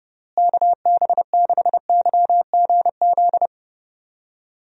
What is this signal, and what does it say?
Morse code, heard as a keyed tone, about 31 wpm, 700 Hz. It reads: KB6YGZ
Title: USA Call Signs - 6 Characters - Repeated 2x - 30wpm